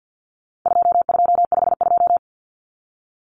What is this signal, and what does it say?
Morse code, heard as a keyed tone, about 50 wpm, 700 Hz. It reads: 2252
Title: Common Cut Numbers - 4-Digit 50wpm